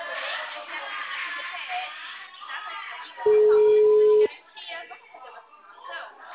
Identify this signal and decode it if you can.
Ringback tone (Europe, 425 Hz), with speech.